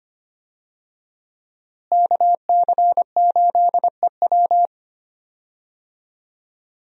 Morse code, heard as a keyed tone, about 25 wpm, 700 Hz. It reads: KC8EW
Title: USA Call Signs - 5 Characters - Repeated 3x - 25wpm